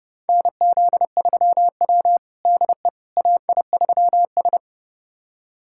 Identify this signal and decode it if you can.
Morse code, heard as a keyed tone, about 32 wpm, 700 Hz. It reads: NZ3W DE AI3S
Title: USA Callsign DE Callsign - 4 Characters - 30wpm